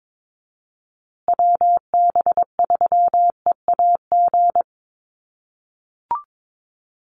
Morse code, heard as a keyed tone, about 22 wpm, 700 Hz. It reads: WB3EAG
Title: USA Call Signs Repeated 3x 22wpm